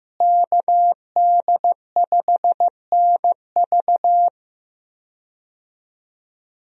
Morse code, heard as a keyed tone, about 15 wpm, 700 Hz. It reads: KD5NV
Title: USA Call Signs - 5 Characters - Repeated 3x - 15wpm